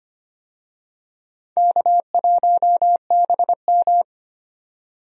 Morse code, heard as a keyed tone, about 25 wpm, 700 Hz. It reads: K1BM
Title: USA Call Signs - 4 Characters - 25wpm